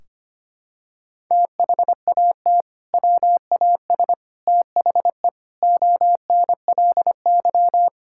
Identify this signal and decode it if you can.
Morse code, heard as a keyed tone, about 25 wpm, 700 Hz. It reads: THAT WAS THE ONLY
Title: Sentences from Top 1600 Words 25wpm